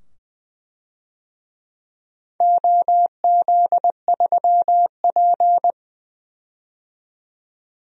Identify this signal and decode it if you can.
Morse code, heard as a keyed tone, about 20 wpm, 700 Hz. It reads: OZ3P
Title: DXCC-100 Call Signs - 4 Characters - 20wpm